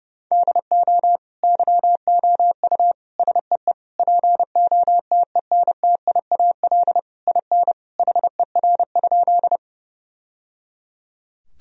Morse code, heard as a keyed tone, about 30 wpm, 700 Hz. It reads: DO YOU SEE POTENTIAL IN HER?
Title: Sentences from Top 1000 Words - Review Part 4 - 30wpm